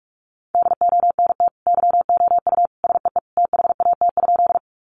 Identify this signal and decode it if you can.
Morse code, heard as a keyed tone, about 44 wpm, 700 Hz. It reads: DONT YOU SEE THAT?
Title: Sentences from Top 300 Words 45wpm